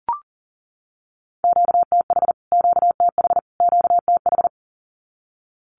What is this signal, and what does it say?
Morse code, heard as a keyed tone, about 40 wpm, 700 Hz. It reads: QTH QTH QTH
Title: Q-Codes Repeated 3x 40wpm